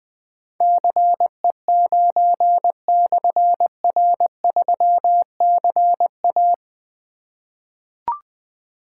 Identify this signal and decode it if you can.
Morse code, heard as a keyed tone, about 20 wpm, 700 Hz. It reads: CE9/R3CA
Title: Difficult Call Signs 20wpm